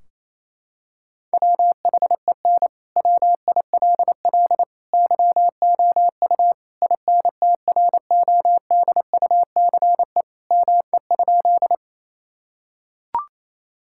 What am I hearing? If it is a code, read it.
Morse code, heard as a keyed tone, about 28 wpm, 700 Hz. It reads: WHEN WILL YOU INTRODUCE ME?
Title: Sentences from Top 1600 Words 28wpm